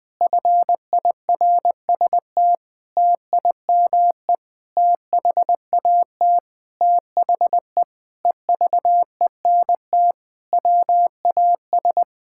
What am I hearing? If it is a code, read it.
Morse code, heard as a keyed tone, about 20 wpm, 700 Hz. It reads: FIRST TIME THAT THE EVENT WAS
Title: Sentences with 8-Word Phrases - 20wpm